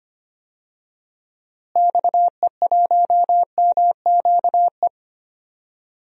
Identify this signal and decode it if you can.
Morse code, heard as a keyed tone, about 25 wpm, 700 Hz. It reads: XE1MQE
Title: DXCC-100 Call Signs - 6 Characters - Repeated 3x - 25wpm